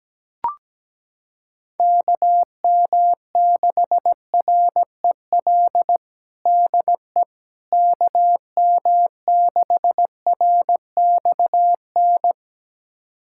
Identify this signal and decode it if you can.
Morse code, heard as a keyed tone, about 17 wpm, 700 Hz. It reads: KM6REL DE KM6RXN
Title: USA Callsign DE Callsign - 6 Characters - 17wpm